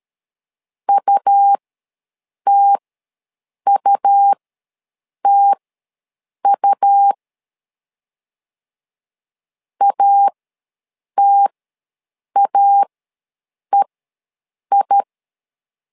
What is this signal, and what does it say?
Morse code, heard as a keyed tone, about 13 wpm, spaced as for 7 wpm, 800 Hz. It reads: UTUTU ATAEI
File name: LEZIONI / 2 UHDN / 2 U ws 12wpm 1655